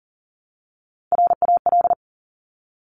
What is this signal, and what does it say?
Morse code, heard as a keyed tone, about 39 wpm, 700 Hz. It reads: RAL
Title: Top Three Letter Combinations in Words 40wpm